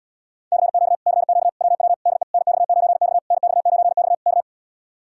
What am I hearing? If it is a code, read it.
Morse code, heard as a keyed tone, about 67 wpm, 700 Hz. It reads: CQ CQ WW DE NX9X NX9X K